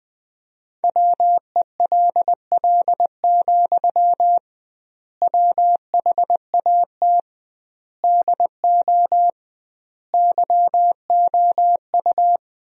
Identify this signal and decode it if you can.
Morse code, heard as a keyed tone, about 20 wpm, 700 Hz. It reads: WELL, WHAT DO YOU
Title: 2x Word-Spacing - Sentences from Top 200 Words - 20wpm